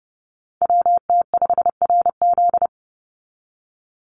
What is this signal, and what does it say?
Morse code, heard as a keyed tone, about 30 wpm, 700 Hz. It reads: WT5RZ
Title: Call Signs Repeated 3x - 30wpm